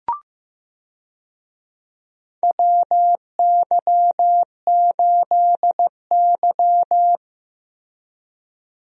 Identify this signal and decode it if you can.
Morse code, heard as a keyed tone, about 15 wpm, 700 Hz. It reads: WY8Y
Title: USA Call Signs - 4 Characters - Repeated 3x - 15wpm